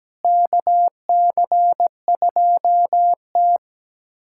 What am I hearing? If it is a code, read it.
Morse code, heard as a keyed tone, about 17 wpm, 700 Hz. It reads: KC2T
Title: USA Callsign DE Callsign - 4 Characters - 17wpm